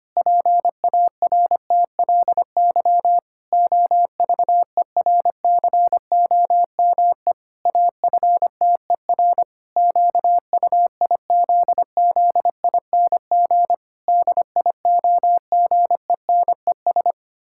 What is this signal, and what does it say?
Morse code, heard as a keyed tone, about 25 wpm, 700 Hz. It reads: PARTLY OVERCOME AFTER QUIZZING DIOGENES
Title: Pangram Challenge v2 - 25wpm